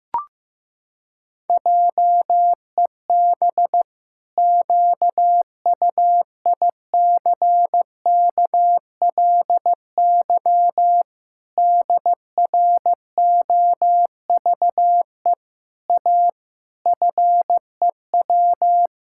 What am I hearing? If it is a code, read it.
Morse code, heard as a keyed tone, about 14 wpm, 700 Hz. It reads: JEB QUICKLY DROVE A FEW
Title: Pangram Challenge v2 - 15wpm